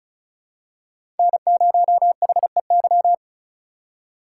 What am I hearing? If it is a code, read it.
Morse code, heard as a keyed tone, about 34 wpm, 700 Hz. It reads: N0HEY
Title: USA Call Signs - 5 Characters - 2x Character Spacing - 35wpm